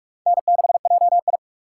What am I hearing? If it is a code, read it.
Morse code, heard as a keyed tone, about 44 wpm, 700 Hz. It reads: N6JI